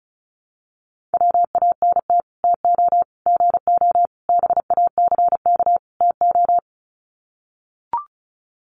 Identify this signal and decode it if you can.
Morse code, heard as a keyed tone, about 34 wpm, 700 Hz. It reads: WANT TO GO BACK TO
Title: Sets of 5 Words 35wpm